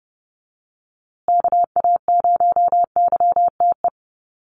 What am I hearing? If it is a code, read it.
Morse code, heard as a keyed tone, about 30 wpm, 700 Hz. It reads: KA0YTE
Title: USA Call Signs Repeated 2x 30wpm